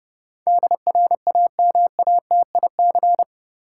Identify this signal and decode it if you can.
Morse code, heard as a keyed tone, about 30 wpm, 700 Hz. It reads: DRAMATIC
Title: Sentences from Top 1300 Words - New Word Drill - 30wpm